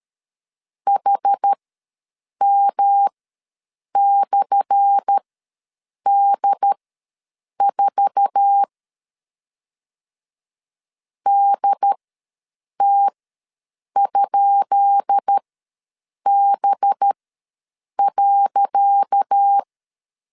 Morse code, heard as a keyed tone, about 13 wpm, spaced as for 7 wpm, 800 Hz. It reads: HM/D4 DT?B.
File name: LEZIONI / 10 SIGNS / 10D COMA ws 12wpm 3618